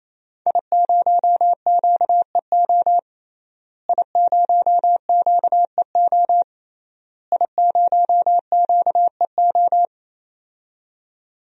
Morse code, heard as a keyed tone, about 28 wpm, 700 Hz. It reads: I0QEO I0QEO I0QEO
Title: DXCC-100 Call Signs - 5 Characters - Repeated 3x - 28wpm